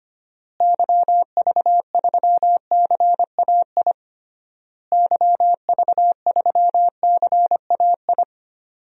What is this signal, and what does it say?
Morse code, heard as a keyed tone, about 25 wpm, 700 Hz. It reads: YV3CAI YV3CAI
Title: DXCC-100 Call Signs - 6 Characters - Repeated 3x - 25wpm